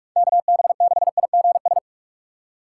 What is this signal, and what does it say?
Morse code, heard as a keyed tone, about 42 wpm, 700 Hz. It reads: KB6IGS
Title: USA Call Signs - 6 Characters - 4x Character Spacing - 45wpm